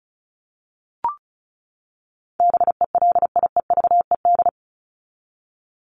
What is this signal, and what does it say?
Morse code, heard as a keyed tone, about 34 wpm, 700 Hz. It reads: BELIEVED
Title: ED Suffix - Top 100 Words with Base Words 35wpm